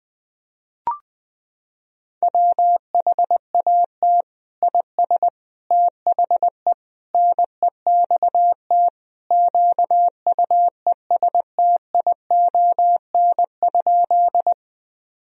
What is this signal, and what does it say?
Morse code, heard as a keyed tone, about 20 wpm, 700 Hz. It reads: WHAT IS THE NEXT QUESTION?
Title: Sentences from Top 1000 Words - Review Part 1 - 20wpm